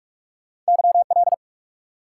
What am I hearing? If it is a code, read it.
Morse code, heard as a keyed tone, about 46 wpm, 700 Hz. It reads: YL
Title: Top 90 QSO Elements 45wpm